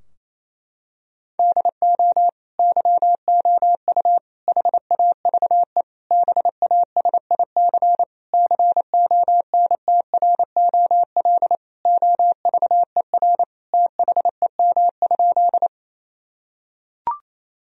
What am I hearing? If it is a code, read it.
Morse code, heard as a keyed tone, about 28 wpm, 700 Hz. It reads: DO YOU HAVE BASIC CONTROL OVER THEM?
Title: Sentences from Top 1000 Words - Review Part 3 - 28wpm